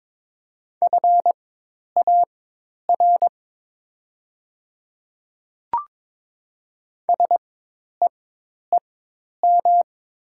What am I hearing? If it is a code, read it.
Morse code, heard as a keyed tone, about 22 wpm, spaced as for 10 wpm, 690 Hz. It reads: FAR SEEM
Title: 4x Character Spacing - Top 300 Words - 22wpm